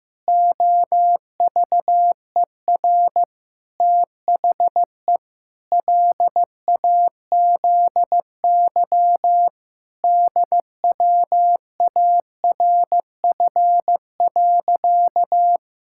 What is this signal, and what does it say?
Morse code, heard as a keyed tone, about 15 wpm, 700 Hz. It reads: OVER THE LAZY DWARF.
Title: Pangram Challenge v2 - 15wpm